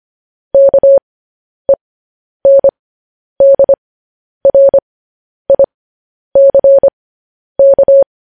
Morse code, heard as a keyed tone, about 25 wpm, spaced as for 10 wpm, 550 Hz. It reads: KENDRICK